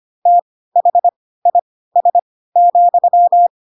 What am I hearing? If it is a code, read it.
Morse code, heard as a keyed tone, about 25 wpm, spaced as for 16 wpm, 700 Hz. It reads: THIS,
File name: Pinocchio / Pinocchio ch3 10 25 ik1hge